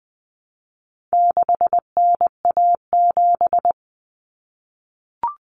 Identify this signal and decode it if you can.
Morse code, heard as a keyed tone, about 20 wpm, 700 Hz. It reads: 6NA7
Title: Common Cut Numbers - 4-Digit 20wpm